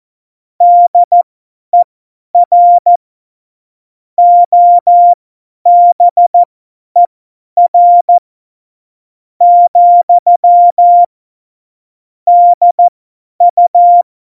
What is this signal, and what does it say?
Morse code, heard as a keyed tone, about 14 wpm, spaced as for 10 wpm, 700 Hz. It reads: DER OBER , DU
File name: Morsekurs / Witze / Witz 03.Dezember 2019, 50 F70